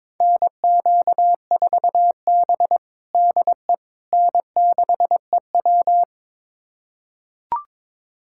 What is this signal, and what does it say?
Morse code, heard as a keyed tone, about 21 wpm, 700 Hz. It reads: NQ4B DE N6EW
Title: USA Callsign DE Callsign - 4 Characters - 22wpm